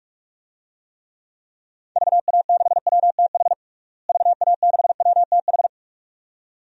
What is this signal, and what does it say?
Morse code, heard as a keyed tone, about 41 wpm, 700 Hz. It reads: VA6WTH VA6WTH
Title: DXCC-100 Call Signs - 6 Characters - Repeated 2x - 45wpm